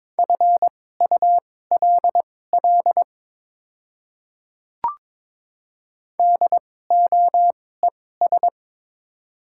Morse code, heard as a keyed tone, about 22 wpm, spaced as for 16 wpm, 700 Hz. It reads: FULL DOES
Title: 2x Character Spacing - Top 500 Words - 22wpm